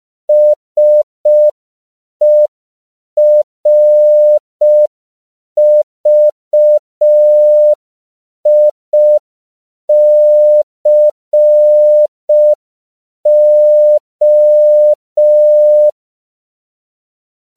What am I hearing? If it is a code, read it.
Morse code, heard as a keyed tone, about 5 wpm, 600 Hz. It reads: SERVICO
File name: CW / 5PPM